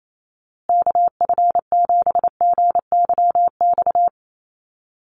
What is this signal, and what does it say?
Morse code, heard as a keyed tone, about 28 wpm, 700 Hz. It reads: KF7GYX
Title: USA Call Signs Repeated 3x 28wpm